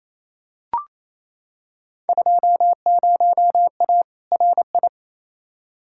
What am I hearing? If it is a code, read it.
Morse code, heard as a keyed tone, about 28 wpm, 700 Hz. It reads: 20A RI
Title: Field Day Exchanges Encore - 28wpm